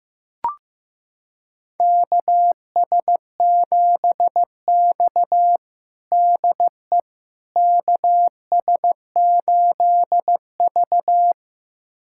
Morse code, heard as a keyed tone, about 15 wpm, 700 Hz. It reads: KS7X DE KS8V
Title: USA Callsign DE Callsign - 4 Characters - 15wpm